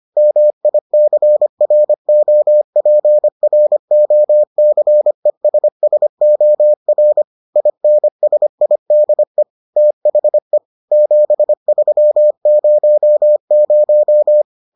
Morse code, heard as a keyed tone, about 25 wpm, 600 Hz. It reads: MICROPROCESSOR INSIDE THE 7300